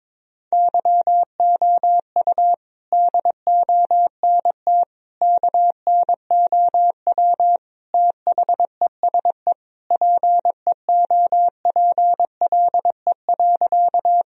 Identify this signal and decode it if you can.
Morse code, heard as a keyed tone, about 22 wpm, 700 Hz. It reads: YOU DONT KNOW THESE PEOPLE.
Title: Sentences from Top 1000 Words - Review Part 1 - 22wpm